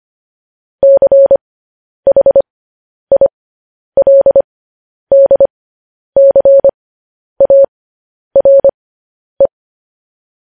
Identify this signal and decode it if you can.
Morse code, heard as a keyed tone, about 25 wpm, spaced as for 10 wpm, 550 Hz. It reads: CHILDCARE
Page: news headlines in morse code at 10 wpm